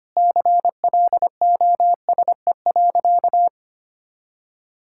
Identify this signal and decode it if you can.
Morse code, heard as a keyed tone, about 25 wpm, 700 Hz. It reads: CLOSE.
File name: practice / 25wpm / Sentences with 2-Word Phrases - 25wpm